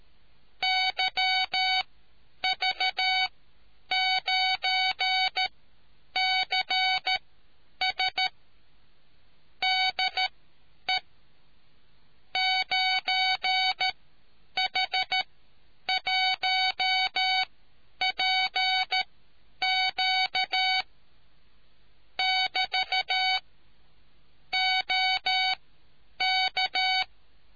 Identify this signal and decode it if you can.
Morse code, heard as a keyed tone, about 13 wpm, spaced as for 9 wpm, 750 Hz. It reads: YV9CS DE 9H1PQ <BT> OK